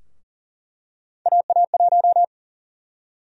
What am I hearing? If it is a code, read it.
Morse code, heard as a keyed tone, about 40 wpm, 700 Hz. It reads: AA1
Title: USA Call Sign Prefixes - 40wpm